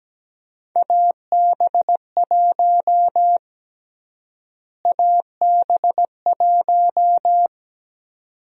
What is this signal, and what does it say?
Morse code, heard as a keyed tone, about 17 wpm, 700 Hz. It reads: AB1 AB1
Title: USA Call Sign Prefixes - Repeated 2x - 17wpm